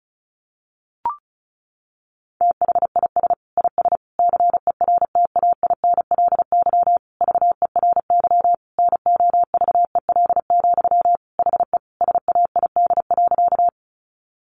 Morse code, heard as a keyed tone, about 34 wpm, 700 Hz. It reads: THIS IS CERTAINLY VERY NOVEL, HE SAID.
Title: Sentences from Top 1300 Words 35wpm